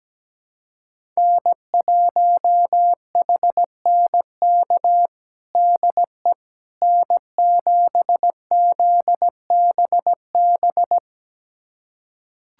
Morse code, heard as a keyed tone, about 17 wpm, 700 Hz. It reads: N1HNK DE N7ZBB
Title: USA Callsign DE Callsign - 5 Characters - 17wpm